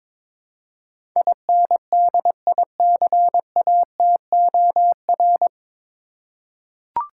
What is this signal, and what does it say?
Morse code, heard as a keyed tone, about 22 wpm, 700 Hz. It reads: INDICATOR
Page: Morse Code Ninja practice 22wpm